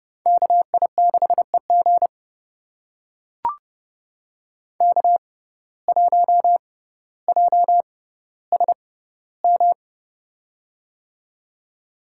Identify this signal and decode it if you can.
Morse code, heard as a keyed tone, about 30 wpm, spaced as for 15 wpm, 700 Hz. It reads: KI6EG K1JSM
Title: USA Call Signs - 5 Characters - 6x Character Spacing - 30wpm